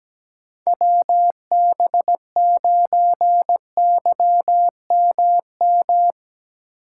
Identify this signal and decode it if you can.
Morse code, heard as a keyed tone, about 17 wpm, 700 Hz. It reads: WB9YMM
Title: USA Call Signs - 6 Characters - 3x Character Spacing - 17wpm